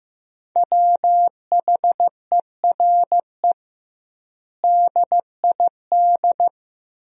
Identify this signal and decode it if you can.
Morse code, heard as a keyed tone, about 15 wpm, 700 Hz. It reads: WHERE DID
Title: 2x Word-Spacing - Sentences from Top 400 Words - 15wpm